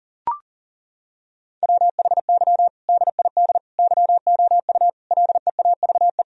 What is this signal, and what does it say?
Morse code, heard as a keyed tone, about 40 wpm, 700 Hz. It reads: WHY DID YOU LEAVE
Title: Sentences from Top 1000 Words - Review Part 3 - 40wpm